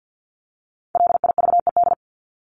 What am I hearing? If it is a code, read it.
Morse code, heard as a keyed tone, about 50 wpm, 700 Hz. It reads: LIVED
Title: ED Suffix - Top 100 Words with Base Words 50wpm